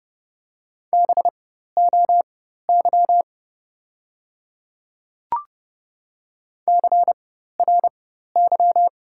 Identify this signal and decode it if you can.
Morse code, heard as a keyed tone, about 30 wpm, spaced as for 14 wpm, 700 Hz. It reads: BOY CRY
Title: 4x Character Spacing - Top 500 Words - 30wpm